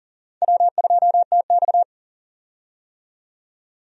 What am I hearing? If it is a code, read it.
Morse code, heard as a keyed tone, about 40 wpm, 700 Hz. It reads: W2TX
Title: USA Call Signs - 4 Characters - 40wpm